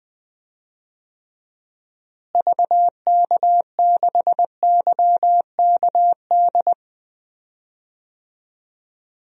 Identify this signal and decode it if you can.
Morse code, heard as a keyed tone, about 20 wpm, 700 Hz. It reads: VK6YKD
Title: DXCC-100 Call Signs - 6 Characters - Repeated 2x - 20wpm